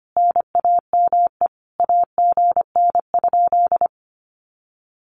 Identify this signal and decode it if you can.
Morse code, heard as a keyed tone, about 25 wpm, 700 Hz. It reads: NAME AGN?
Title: Common QSO Phrases Encore 25wpm